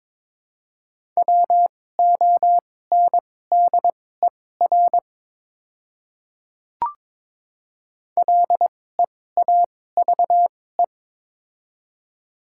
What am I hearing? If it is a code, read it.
Morse code, heard as a keyed tone, about 22 wpm, spaced as for 16 wpm, 700 Hz. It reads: WONDER LEAVE
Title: 2x Character Spacing - Top 500 Words - 22wpm